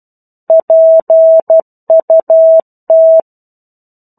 Morse code, heard as a keyed tone, about 12 wpm, 650 Hz. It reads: PUT
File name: 01-Chapter-01 12 WPM